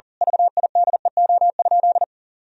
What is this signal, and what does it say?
Morse code, heard as a keyed tone, about 40 wpm, 700 Hz. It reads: VIDEO?
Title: Sentences from Top 1000 Words - Review Part 4 - 40wpm